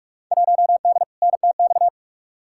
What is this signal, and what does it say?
Morse code, heard as a keyed tone, about 48 wpm, 700 Hz. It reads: 1D NTX
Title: Field Day Exchanges - 45wpm